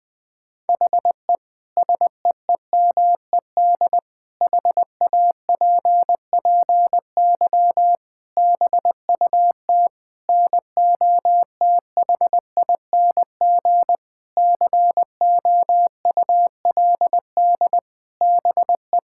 Morse code, heard as a keyed tone, about 20 wpm, 700 Hz. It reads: HE SEEMED HAPPY BUT NOTHING COULD BE
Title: Sentences with 8-Word Phrases - 20wpm